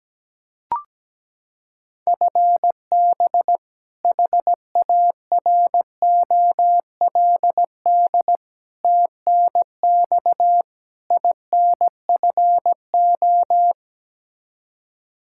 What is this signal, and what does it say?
Morse code, heard as a keyed tone, about 17 wpm, 700 Hz. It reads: FB HAROLD TNX INFO
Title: Practice QSO - Exchange 3 - More Info - 17wpm